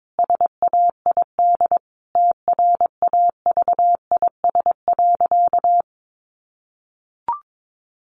Morse code, heard as a keyed tone, about 22 wpm, 700 Hz. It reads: SAID TRAVIS.
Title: A Sound Of Thunder 22wpm